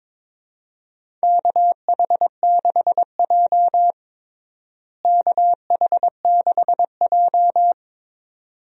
Morse code, heard as a keyed tone, about 22 wpm, 700 Hz. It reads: KH6J KH6J
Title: DXCC-100 Call Signs - 4 Characters - Repeated 2x - 22wpm